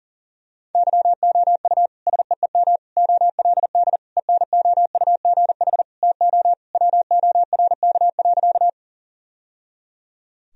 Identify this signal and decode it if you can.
Morse code, heard as a keyed tone, about 40 wpm, 700 Hz. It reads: YOU SEEM OLD ENOUGH TO WORK.
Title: Sentences from Top 1000 Words - Review Part 1 - 40wpm